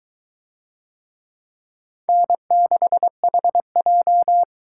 Morse code, heard as a keyed tone, about 22 wpm, 700 Hz. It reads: N6HJ